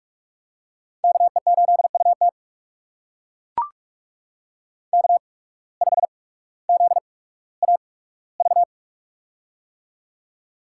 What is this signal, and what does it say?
Morse code, heard as a keyed tone, about 44 wpm, spaced as for 19 wpm, 690 Hz. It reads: KE8UT K5ZAV
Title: USA Call Signs - 5 Characters - 8x Character Spacing - 45wpm